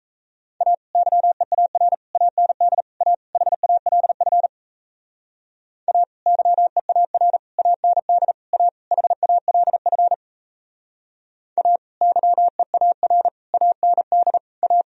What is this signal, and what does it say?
Morse code, heard as a keyed tone, about 37 wpm, 700 Hz. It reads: A YEAR AND A HALF A YEAR AND A HALF A YEAR AND A